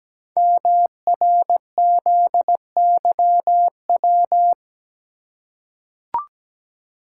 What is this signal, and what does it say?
Morse code, heard as a keyed tone, about 17 wpm, 700 Hz. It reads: MRZYW
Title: Code Groups - 5 Letters - 17wpm